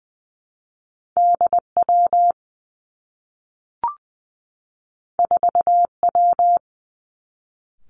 Morse code, heard as a keyed tone, about 20 wpm, 700 Hz. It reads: D W 4 W
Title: Two Challenging Letter-Number Combinations 20wpm